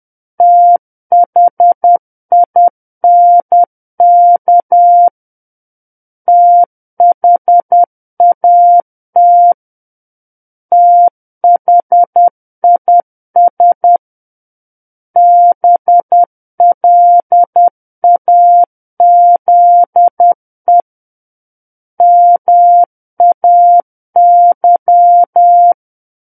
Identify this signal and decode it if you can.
Morse code, heard as a keyed tone, about 10 wpm, 700 Hz. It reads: THINK THAT THIS BLAZE MAY
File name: Book One / 01-Chapter 1 - 10 WPM